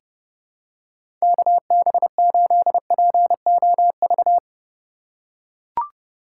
Morse code, heard as a keyed tone, about 29 wpm, 700 Hz. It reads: KB8POV
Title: USA Call Signs - 6 Characters - 4x Character Spacing - 30wpm